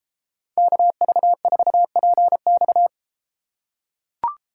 Morse code, heard as a keyed tone, about 32 wpm, 700 Hz. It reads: KV4PX